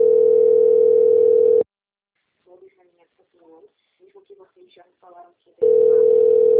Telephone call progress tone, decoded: ringback tone

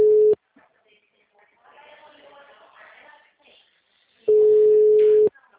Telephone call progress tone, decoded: ringback tone

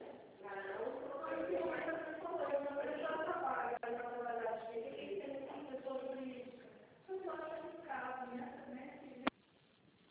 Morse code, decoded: none

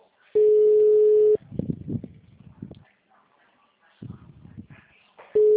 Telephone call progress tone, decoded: ringback tone

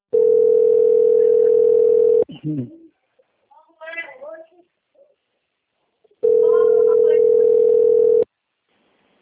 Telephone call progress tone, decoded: ringback tone

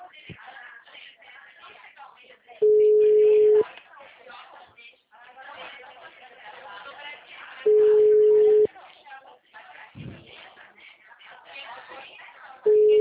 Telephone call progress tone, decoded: ringback tone

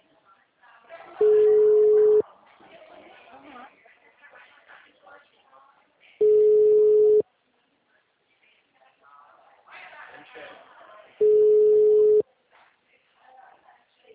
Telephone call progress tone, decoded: ringback tone

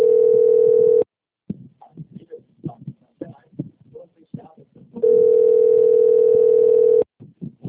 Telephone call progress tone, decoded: ringback tone

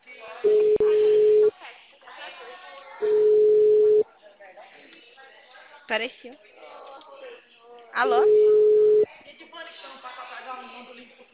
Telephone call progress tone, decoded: ringback tone

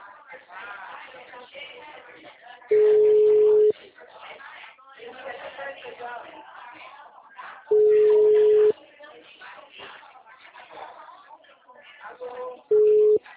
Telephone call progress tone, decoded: ringback tone

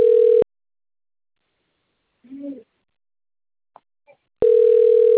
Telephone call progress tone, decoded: ringback tone

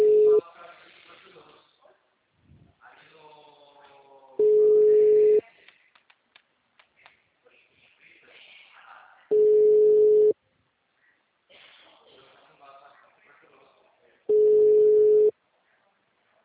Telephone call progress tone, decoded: ringback tone

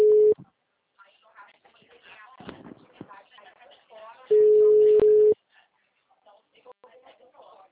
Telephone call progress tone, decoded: ringback tone

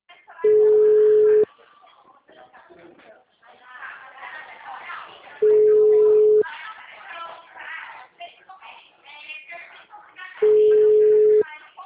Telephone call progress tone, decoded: ringback tone

